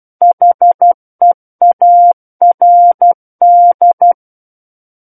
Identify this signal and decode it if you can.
Morse code, heard as a keyed tone, about 12 wpm, 700 Hz. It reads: HEARD
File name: Book One / 03-Chapter 3 - 12 WPM